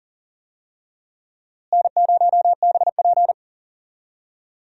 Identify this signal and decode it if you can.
Morse code, heard as a keyed tone, about 37 wpm, 700 Hz. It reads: N0BP
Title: USA Call Signs - 4 Characters - Repeated 3x - 40wpm